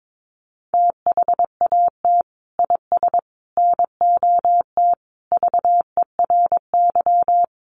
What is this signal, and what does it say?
Morse code, heard as a keyed tone, about 22 wpm, 700 Hz. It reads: THAT IS NOT VERY
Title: Sentences from Top 600 Words 22wpm